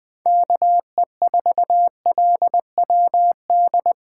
Morse code, heard as a keyed tone, about 20 wpm, 700 Hz. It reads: KE4LWD
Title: USA Call Signs - 6 Characters - 2x Character Spacing - 20wpm